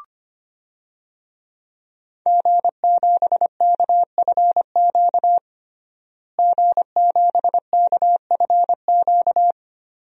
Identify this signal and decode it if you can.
Morse code, heard as a keyed tone, about 25 wpm, 700 Hz. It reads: G7KFQ G7KFQ
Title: DXCC-100 Call Signs - 5 Characters - Repeated 2x - 25wpm